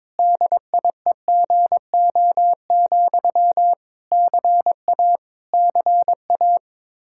Morse code, heard as a keyed tone, about 22 wpm, 700 Hz. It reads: DIEGO, CA CA
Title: Practice QSO - Exchange 1 - Beginning - 22wpm